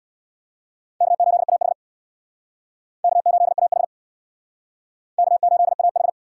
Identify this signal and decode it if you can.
Morse code, heard as a keyed tone, about 62 wpm, 700 Hz. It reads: K8AH K8AH K8AH
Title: Speed Racing - USA 1x2 Call Signs 50wpm